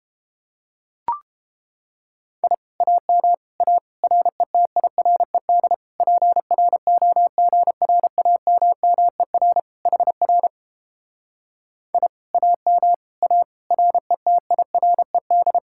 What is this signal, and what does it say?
Morse code, heard as a keyed tone, about 29 wpm, 700 Hz. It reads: I AM A RETIRED PROGRAMMER HR I AM A RETIRED